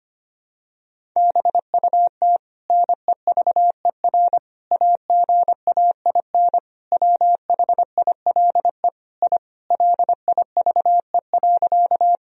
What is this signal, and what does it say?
Morse code, heard as a keyed tone, about 25 wpm, 700 Hz. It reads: BUT NEVER AGAIN WHILE I LIVE.
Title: Sentences from Top 1000 Words - Review Part 1 - 25wpm